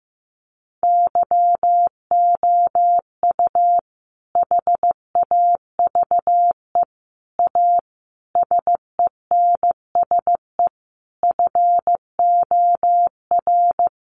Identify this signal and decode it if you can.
Morse code, heard as a keyed tone, about 15 wpm, 700 Hz. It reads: YOU HAVE A SENSE FOR
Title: Sentences from Top 700 Words 15wpm